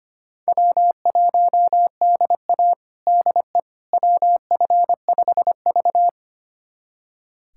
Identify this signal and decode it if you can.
Morse code, heard as a keyed tone, about 25 wpm, 700 Hz. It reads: W1DA DE WF5V
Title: USA Callsign DE Callsign - 4 Characters - 25wpm